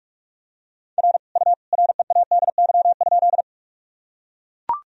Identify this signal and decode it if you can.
Morse code, heard as a keyed tone, about 42 wpm, 700 Hz. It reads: R U READY?
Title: Sentences from Top 400 Words and Abbreviations 45wpm